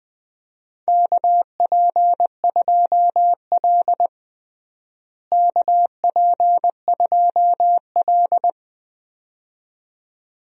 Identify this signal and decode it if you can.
Morse code, heard as a keyed tone, about 20 wpm, 700 Hz. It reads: KP2L KP2L
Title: DXCC-100 Call Signs - 4 Characters - Repeated 2x - 20wpm